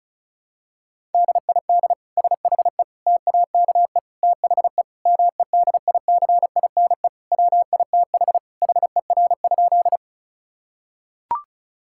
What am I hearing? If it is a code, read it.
Morse code, heard as a keyed tone, about 36 wpm, 700 Hz. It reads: DID SHE TAKE THE MEDICINE WITH HER?
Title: Sentences from Top 1000 Words - Review Part 6 - 35wpm